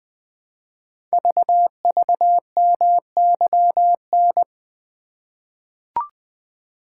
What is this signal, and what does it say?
Morse code, heard as a keyed tone, about 20 wpm, 700 Hz. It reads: VVMYN
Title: Code Groups - 5 Letters - 20wpm